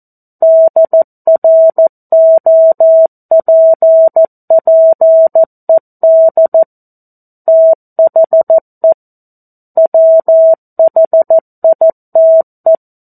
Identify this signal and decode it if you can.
Morse code, heard as a keyed tone, about 14 wpm, 650 Hz. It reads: DROPPED THE WHITE
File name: 02-Chapter-02 14 WPM